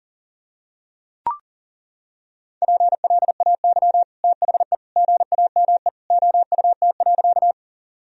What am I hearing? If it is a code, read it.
Morse code, heard as a keyed tone, about 37 wpm, 700 Hz. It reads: PLAY THE GAME OUT.
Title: Sentences from Top 1000 Words - Review Part 3 - 40wpm